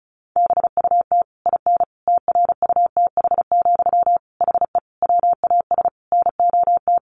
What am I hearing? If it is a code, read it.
Morse code, heard as a keyed tone, about 34 wpm, 700 Hz. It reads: BUT IN TRUTH, HE WAS NOT
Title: Sentences from Top 1500 Words 35wpm